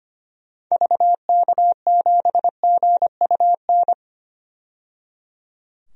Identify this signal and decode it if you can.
Morse code, heard as a keyed tone, about 25 wpm, 700 Hz. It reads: VK7GUN
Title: DXCC-100 Call Signs - 6 Characters - 25wpm